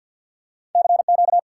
Morse code, heard as a keyed tone, about 47 wpm, 700 Hz. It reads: CQ
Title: Top 90 QSO Elements 50wpm